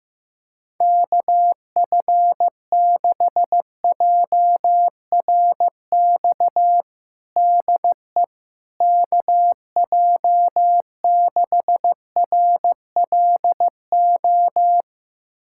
Morse code, heard as a keyed tone, about 15 wpm, 700 Hz. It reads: KF6JRX DE KJ6RLO
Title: USA Callsign DE Callsign 15wpm